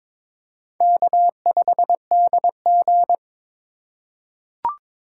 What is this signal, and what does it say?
Morse code, heard as a keyed tone, about 22 wpm, 700 Hz. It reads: K5DG
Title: USA Call Signs - 4 Characters - 6x Character Spacing - 22wpm